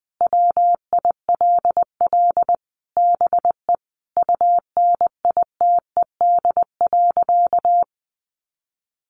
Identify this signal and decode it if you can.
Morse code, heard as a keyed tone, about 20 wpm, 700 Hz. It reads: WILL BE UNITED.
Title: Sentences from Top 600 Words 20wpm